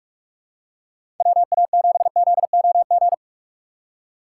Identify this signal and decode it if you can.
Morse code, heard as a keyed tone, about 46 wpm, 700 Hz. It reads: WA7ZOG
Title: USA Call Signs - 6 Characters - 3x Character Spacing - 45wpm